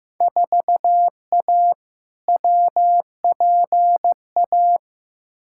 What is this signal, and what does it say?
Morse code, heard as a keyed tone, about 15 wpm, 700 Hz. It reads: 4A WPA
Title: Field Day Exchanges Encore - 15wpm